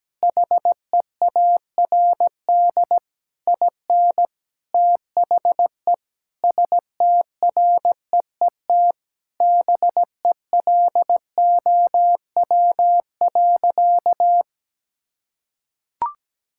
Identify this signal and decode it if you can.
Morse code, heard as a keyed tone, about 17 wpm, 700 Hz. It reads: HEARD IN THE STREET BELOW.
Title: Sentences from Top 1000 Words - Review Part 5 - 17wpm